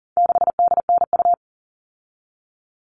Morse code, heard as a keyed tone, about 40 wpm, 700 Hz. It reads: 6DNU
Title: All Cut Numbers - 4-Digit 40wpm